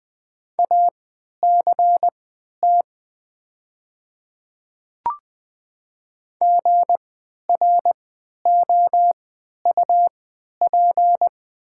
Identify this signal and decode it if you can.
Morse code, heard as a keyed tone, about 20 wpm, spaced as for 11 wpm, 700 Hz. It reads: ACT GROUP
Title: 3x Character Spacing - Top 300 Words - 20wpm